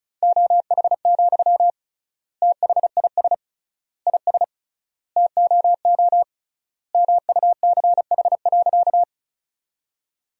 Morse code, heard as a keyed tone, about 36 wpm, 700 Hz. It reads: OH, THIS IS TOO MUCH.
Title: 3x Word-Spacing - Sentences from Top 500 Words - 35wpm